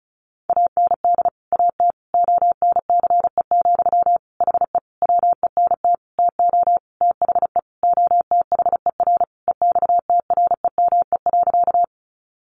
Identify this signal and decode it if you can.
Morse code, heard as a keyed tone, about 34 wpm, 700 Hz. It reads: AND AT ONCE, HE WENT TO THE OTHER EXTREME.
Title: Sentences from Top 1300 Words 35wpm